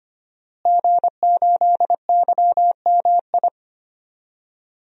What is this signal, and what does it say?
Morse code, heard as a keyed tone, about 25 wpm, 700 Hz. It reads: G8YMI
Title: DXCC-100 Call Signs - 5 Characters - Repeated 3x - 25wpm